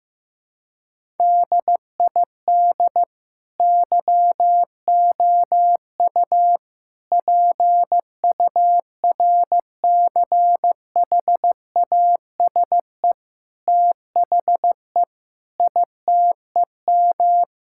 Morse code, heard as a keyed tone, about 16 wpm, 700 Hz. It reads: DID YOU PURCHASE THE ITEM
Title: Sentences from Top 1000 Words - Review Part 6 - 15wpm